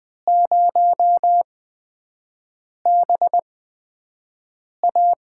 Morse code, heard as a keyed tone, about 20 wpm, spaced as for 5 wpm, 700 Hz. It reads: 0BA
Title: USA Call Signs - 5 Characters - 8x Character Spacing - 20wpm